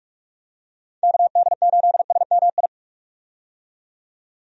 Morse code, heard as a keyed tone, about 47 wpm, 700 Hz. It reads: KD8SMI
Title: USA Call Signs - 6 Characters - 45wpm